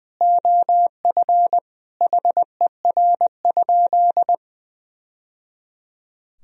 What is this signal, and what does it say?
Morse code, heard as a keyed tone, about 20 wpm, 700 Hz. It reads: OF HER?
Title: Sentences from Top 1000 Words - Review Part 2 - 20wpm